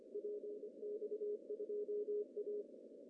Morse code, heard as a keyed tone, about 26 wpm, 420 Hz. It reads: PX2A
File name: CW / cqww-cw-k5tr